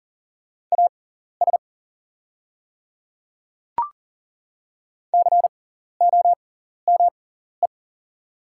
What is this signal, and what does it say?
Morse code, heard as a keyed tone, about 40 wpm, spaced as for 14 wpm, 690 Hz. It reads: AS COME
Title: 6x Character Spacing - Top 100 Words - 40wpm